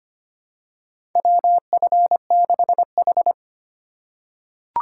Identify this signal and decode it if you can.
Morse code, heard as a keyed tone, about 25 wpm, 700 Hz. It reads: WF6H
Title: USA Call Signs - 4 Characters - 3x Character Spacing - 25wpm